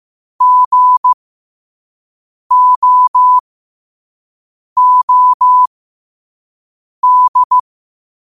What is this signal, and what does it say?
Morse code, heard as a keyed tone, about 15 wpm, spaced as for 5 wpm, 1000 Hz. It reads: GOOD
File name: qotdfiles / 5wpm